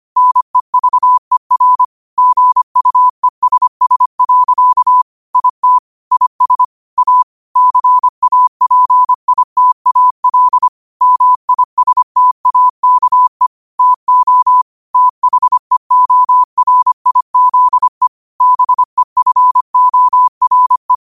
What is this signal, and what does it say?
Morse code, heard as a keyed tone, about 25 wpm, 1000 Hz. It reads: NEVER GUESS. IT IS A CAPITAL MISTAKE TO THEORIZE BEFORE